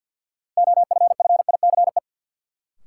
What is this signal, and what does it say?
Morse code, heard as a keyed tone, about 50 wpm, 700 Hz. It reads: OFFICE
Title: Sentences from Top 1000 Words - New Word Drill - 50wpm